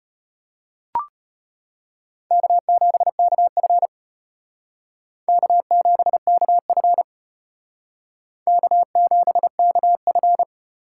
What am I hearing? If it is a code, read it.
Morse code, heard as a keyed tone, about 33 wpm, 700 Hz. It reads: K7KF K7KF K7KF